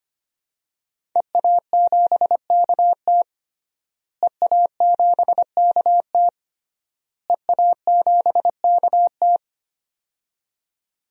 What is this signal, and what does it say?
Morse code, heard as a keyed tone, about 25 wpm, 700 Hz. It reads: EA7KT EA7KT EA7KT
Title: DXCC-100 Call Signs - 5 Characters - Repeated 3x - 25wpm